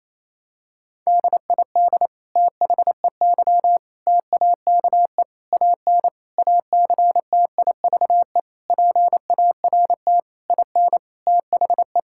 Morse code, heard as a keyed tone, about 28 wpm, 700 Hz. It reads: DID THEY TAKE AN ACTIVE PART IN THE
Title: Sentences from Top 1000 Words - Review Part 6 - 28wpm